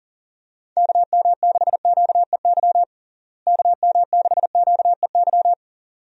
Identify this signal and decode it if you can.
Morse code, heard as a keyed tone, about 40 wpm, 700 Hz. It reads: KM6QEY KM6QEY
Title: USA Call Signs - 6 Characters - Repeated 2x - 40wpm